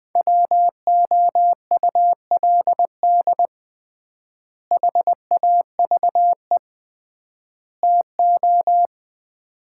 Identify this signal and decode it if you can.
Morse code, heard as a keyed tone, about 21 wpm, 700 Hz. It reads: WOULD HAVE TO
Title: 3x Word-Spacing - Sets of 3 Words - 20wpm